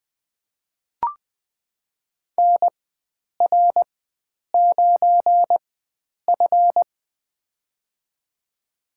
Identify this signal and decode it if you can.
Morse code, heard as a keyed tone, about 20 wpm, spaced as for 9 wpm, 700 Hz. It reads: NR9F
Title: USA Call Signs - 4 Characters - 4x Character Spacing - 20wpm